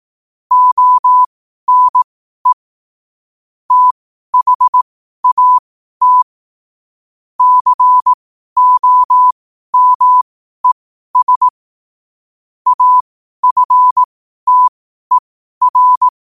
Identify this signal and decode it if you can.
Morse code, heard as a keyed tone, about 18 wpm, spaced as for 12 wpm, 1000 Hz. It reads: ONE THAT COMES AFTER
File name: qotdfiles / 12wpm